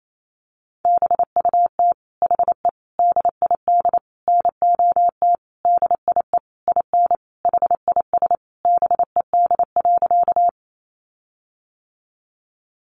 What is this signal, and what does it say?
Morse code, heard as a keyed tone, about 28 wpm, 700 Hz. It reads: BUT HE DID NOT DIE IN HIS BED.
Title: Sentences from Top 1300 Words 28wpm